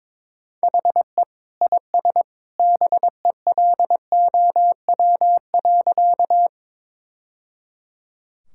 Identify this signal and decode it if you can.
Morse code, heard as a keyed tone, about 22 wpm, 700 Hz. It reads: HE IS BELOW.
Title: Sentences from Top 1000 Words - Review Part 5 - 22wpm